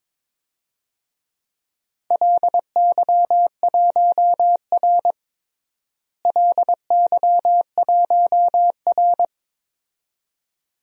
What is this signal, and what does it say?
Morse code, heard as a keyed tone, about 22 wpm, 700 Hz. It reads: LY1R LY1R
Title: DXCC-100 Call Signs - 4 Characters - Repeated 2x - 22wpm